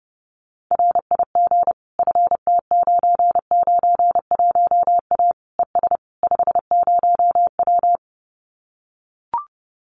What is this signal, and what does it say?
Morse code, heard as a keyed tone, about 30 wpm, 700 Hz. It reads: RIG FT991A ES 50W
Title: Common QSO Phrases Encore 30wpm